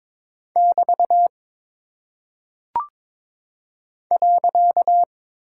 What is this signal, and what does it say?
Morse code, heard as a keyed tone, about 22 wpm, 700 Hz. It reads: <BT> .
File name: practice / 22wpm / Single Punctuation-Prosign - 22wpm